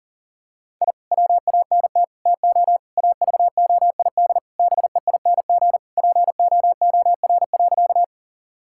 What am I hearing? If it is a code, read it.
Morse code, heard as a keyed tone, about 40 wpm, 690 Hz. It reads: I WANT TO AVOID BEING POOR.
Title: Sentences from Top 1000 Words - Review Part 6 - 40wpm